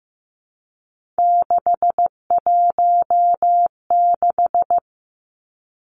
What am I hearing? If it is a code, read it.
Morse code, heard as a keyed tone, about 15 wpm, 700 Hz. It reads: 616
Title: Three Challenging Letter-Number Combinations 15wpm